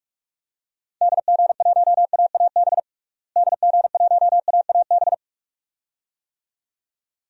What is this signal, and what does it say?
Morse code, heard as a keyed tone, about 46 wpm, 700 Hz. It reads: DG1AAB DG1AAB
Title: DXCC-100 Call Signs - 6 Characters - Repeated 2x - 45wpm